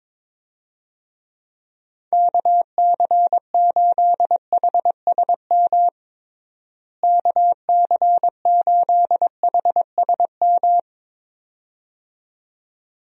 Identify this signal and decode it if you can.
Morse code, heard as a keyed tone, about 22 wpm, 700 Hz. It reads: KC8HSM KC8HSM
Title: USA Call Signs - 6 Characters - Repeated 2x - 22wpm